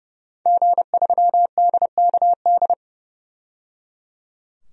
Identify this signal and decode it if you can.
Morse code, heard as a keyed tone, about 30 wpm, 700 Hz. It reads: G3DKD
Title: DXCC-100 Call Signs - 5 Characters - 30wpm